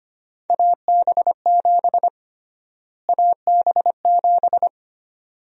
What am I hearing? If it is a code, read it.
Morse code, heard as a keyed tone, about 25 wpm, 700 Hz. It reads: AB7 AB7
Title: USA Call Sign Prefixes - Repeated 2x - 25wpm